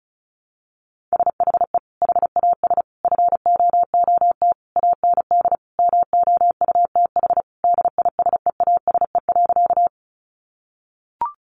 Morse code, heard as a keyed tone, about 34 wpm, 700 Hz. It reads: SHE HAS FOOT AND MOUTH DISEASE.
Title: Sentences from Top 1500 Words 35wpm